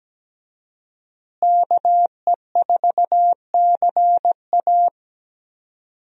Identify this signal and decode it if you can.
Morse code, heard as a keyed tone, about 17 wpm, 700 Hz. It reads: KE4CA
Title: USA Call Signs - 5 Characters - Repeated 2x - 17wpm